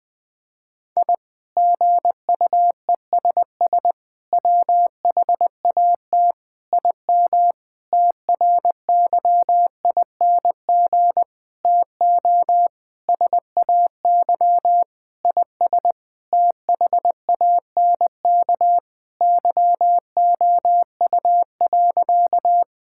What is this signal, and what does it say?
Morse code, heard as a keyed tone, about 20 wpm, 700 Hz. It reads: I GUESS WHAT IM TRYING TO SAY IS THANK YOU.
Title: Sentences with 8-Word Phrases - 20wpm